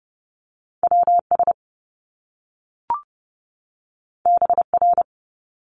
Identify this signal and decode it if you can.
Morse code, heard as a keyed tone, about 30 wpm, 700 Hz. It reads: WS BR
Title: Top Two Letter Combinations in Words 30wpm